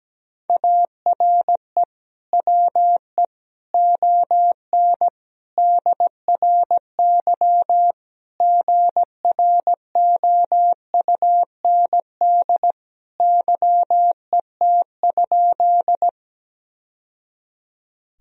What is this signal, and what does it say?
Morse code, heard as a keyed tone, about 17 wpm, 700 Hz. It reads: ARE WE ON DRY GROUND YET?
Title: Sentences from Top 1000 Words - Review Part 3 - 17wpm